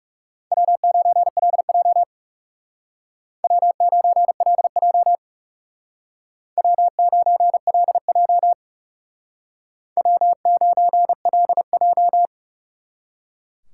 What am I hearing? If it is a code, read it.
Morse code, heard as a keyed tone, about 36 wpm, 700 Hz. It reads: W9LJ W9LJ W9LJ W9LJ
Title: Speed Racing - USA 1x2 Call Signs 30wpm